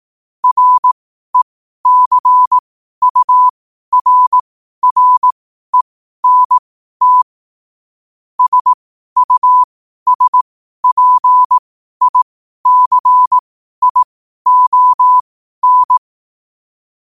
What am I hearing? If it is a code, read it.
Morse code, heard as a keyed tone, about 18 wpm, spaced as for 12 wpm, 1000 Hz. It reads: RECURRENT SUSPICION
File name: qotdfiles / 12wpm